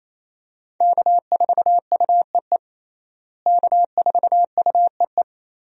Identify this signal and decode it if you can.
Morse code, heard as a keyed tone, about 28 wpm, 700 Hz. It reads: K4UEE K4UEE
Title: USA Call Signs - 5 Characters - Repeated 3x - 28wpm